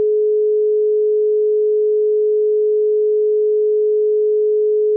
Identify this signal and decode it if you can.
Dial tone (Europe, 425 Hz).
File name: Motorola / dialtone cept